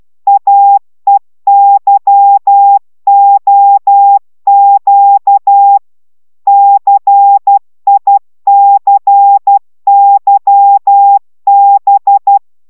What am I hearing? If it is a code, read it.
Morse code, heard as a keyed tone, about 12 wpm, 800 Hz. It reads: AEYOQ CICYB